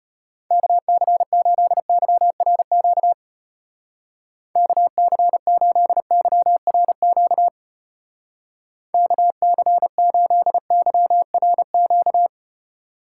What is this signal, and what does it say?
Morse code, heard as a keyed tone, about 32 wpm, 700 Hz. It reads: KC8YRQ KC8YRQ KC8YRQ